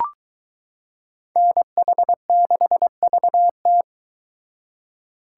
Morse code, heard as a keyed tone, about 22 wpm, 700 Hz. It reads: NH6VT